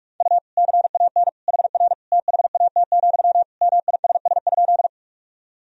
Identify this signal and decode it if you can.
Morse code, heard as a keyed tone, about 42 wpm, 700 Hz. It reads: U CAN HR THAT, MISS?
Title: Sentences from Top 500 Words and Abbreviations 45wpm